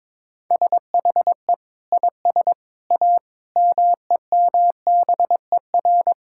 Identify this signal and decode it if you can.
Morse code, heard as a keyed tone, about 22 wpm, 700 Hz. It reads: SHE IS A MEMBER
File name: practice / 22wpm / Sentences with 3-Word Phrases - 22wpm